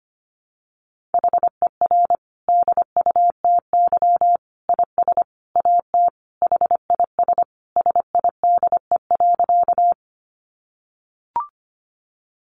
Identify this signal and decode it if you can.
Morse code, heard as a keyed tone, about 25 wpm, 700 Hz. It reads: HER DUTY IS AT HIS SIDE.
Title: Sentences from Top 1500 Words 25wpm